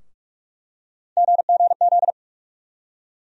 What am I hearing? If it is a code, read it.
Morse code, heard as a keyed tone, about 44 wpm, 700 Hz. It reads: GGZ
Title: USA Call Sign Postfixes - 3 Letters 45wpm